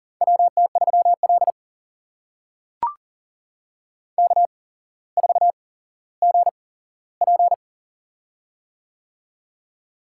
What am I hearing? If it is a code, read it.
Morse code, heard as a keyed tone, about 40 wpm, spaced as for 17 wpm, 700 Hz. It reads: WT3L K4GP
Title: USA Call Signs - 8x Character Spacing - 40wpm